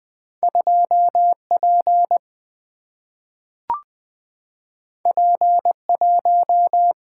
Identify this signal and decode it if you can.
Morse code, heard as a keyed tone, about 20 wpm, 700 Hz. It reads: 2P P1
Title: Single-Letter Single-Number 20wpm